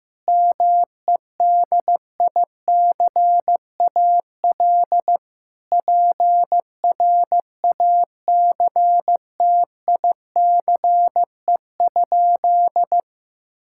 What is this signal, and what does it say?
Morse code, heard as a keyed tone, about 16 wpm, 700 Hz. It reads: MEDICAL PRACTICE?
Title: Sentences from Top 1000 Words - Review Part 5 - 15wpm